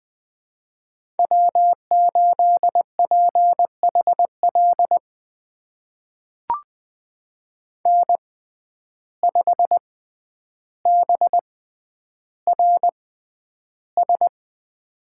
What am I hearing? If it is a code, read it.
Morse code, heard as a keyed tone, about 21 wpm, spaced as for 10 wpm, 700 Hz. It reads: W8PHL N5BRS
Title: USA Call Signs - 5 Characters - 6x Character Spacing - 20wpm